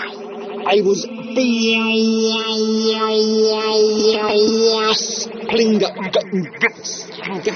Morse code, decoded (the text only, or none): none